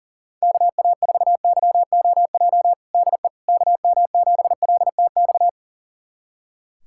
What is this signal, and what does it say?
Morse code, heard as a keyed tone, about 40 wpm, 700 Hz. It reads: KA4YOJ DE KM7LTX
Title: USA Callsign DE Callsign 40wpm